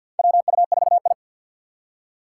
Morse code, heard as a keyed tone, about 49 wpm, 700 Hz. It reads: WU4I
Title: USA Call Signs - 4 Characters - 3x Character Spacing - 50wpm